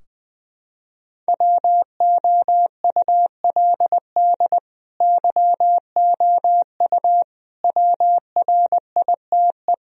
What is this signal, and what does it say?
Morse code, heard as a keyed tone, about 20 wpm, 700 Hz. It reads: WOULD YOU WRITE
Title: Sentences from Top 1000 Words - Review Part 2 - 20wpm